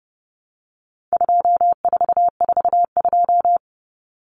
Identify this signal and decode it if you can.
Morse code, heard as a keyed tone, about 30 wpm, 700 Hz. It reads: 2442
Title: Numbers - 4-Digit 30wpm